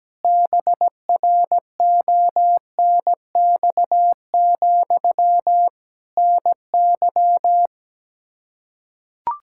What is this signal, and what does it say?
Morse code, heard as a keyed tone, about 17 wpm, 700 Hz. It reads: BRONX, NY
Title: Practice QSO - Exchange 1 - Beginning - 17wpm